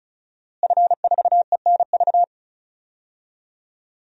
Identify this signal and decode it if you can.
Morse code, heard as a keyed tone, about 36 wpm, 700 Hz. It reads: F4ENV
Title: DXCC-100 Call Signs - 5 Characters - Repeated 2x - 35wpm